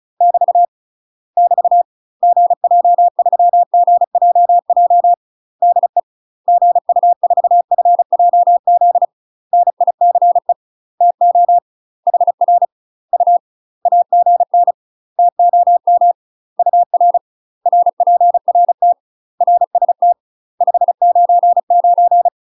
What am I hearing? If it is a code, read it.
Morse code, heard as a keyed tone, about 36 wpm, 700 Hz. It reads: <BT> <BT> GJ3GJJ DE GU4FJZ NICE TO HR U AGN TOM UR RPRT RST 599
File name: UK MASTER QSOs TRAINING / 35WPM / UK MASTER QSOs 0015